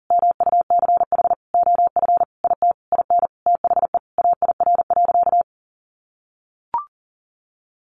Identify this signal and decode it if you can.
Morse code, heard as a keyed tone, about 40 wpm, 700 Hz. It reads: MUCH OF IT IN THE AIR.
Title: Sentences from Top 900 Words 40wpm